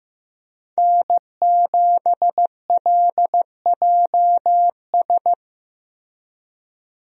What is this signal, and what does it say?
Morse code, heard as a keyed tone, about 15 wpm, 700 Hz. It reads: N7LJS
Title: USA Call Signs - 5 Characters - Repeated 2x - 15wpm